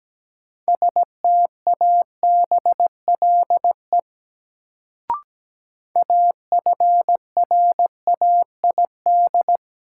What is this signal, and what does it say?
Morse code, heard as a keyed tone, about 17 wpm, 700 Hz. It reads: STABLE AFRAID
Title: Sentences from Top 1500 Words - New Word Drill Alt - 17wpm